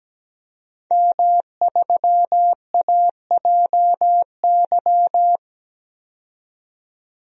Morse code, heard as a keyed tone, about 17 wpm, 700 Hz. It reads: M3AJY
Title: DXCC-100 Call Signs - 5 Characters - 17wpm